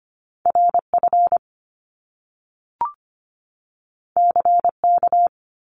Morse code, heard as a keyed tone, about 25 wpm, 700 Hz. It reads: RF CK
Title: Top Two Letter Combinations in Words 25wpm